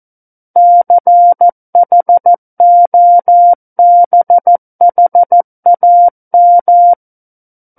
Morse code, heard as a keyed tone, about 14 wpm, 700 Hz. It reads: CHOBHAM
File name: Book One / 05-Chapter 5 - 14 WPM